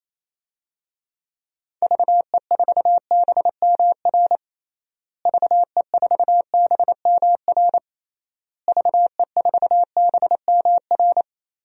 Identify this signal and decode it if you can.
Morse code, heard as a keyed tone, about 28 wpm, 700 Hz. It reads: VE4BMR VE4BMR VE4BMR
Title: DXCC-100 Call Signs - 6 Characters - Repeated 3x - 28wpm